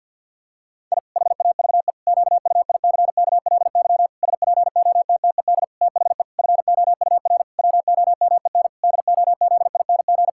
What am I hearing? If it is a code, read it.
Morse code, heard as a keyed tone, about 48 wpm, 700 Hz. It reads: I HAVE QUICKLY SPOTTED THE FOUR WOMEN DOZING
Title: Pangram Challenge v2 - 50wpm